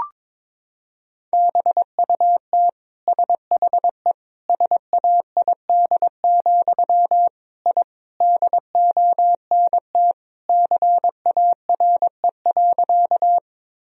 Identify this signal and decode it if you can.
Morse code, heard as a keyed tone, about 21 wpm, 700 Hz. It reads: BUT SHE SAID, I DONT CARE.
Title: Sentences from Top 1000 Words - Review Part 1 - 22wpm